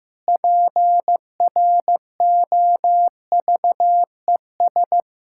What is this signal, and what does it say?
Morse code, heard as a keyed tone, about 15 wpm, 700 Hz. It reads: PROVES
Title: Pangram Challenge v2 - 15wpm